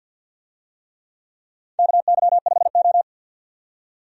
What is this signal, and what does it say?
Morse code, heard as a keyed tone, about 53 wpm, 700 Hz. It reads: KY5O
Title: USA Call Signs - 4 Characters - 50wpm